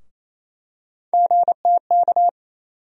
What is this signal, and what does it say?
Morse code, heard as a keyed tone, about 28 wpm, 700 Hz. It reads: GTK
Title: USA Call Sign Postfixes - 3 Letters 28wpm